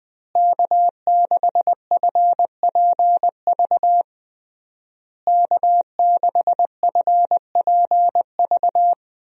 Morse code, heard as a keyed tone, about 20 wpm, 700 Hz. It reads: K6FPV K6FPV
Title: USA Call Signs - 5 Characters - Repeated 3x - 20wpm